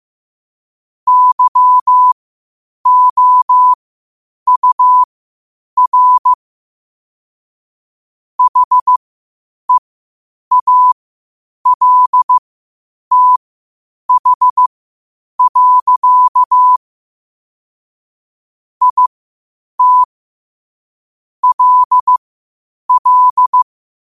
Morse code, heard as a keyed tone, about 15 wpm, spaced as for 8 wpm, 1000 Hz. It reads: YOUR HEALTH. IT LL
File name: qotdfiles / 8wpm